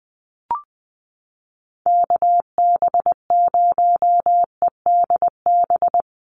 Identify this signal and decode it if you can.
Morse code, heard as a keyed tone, about 20 wpm, 700 Hz. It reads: KB0EDB
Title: USA Call Signs Repeated 3x 20wpm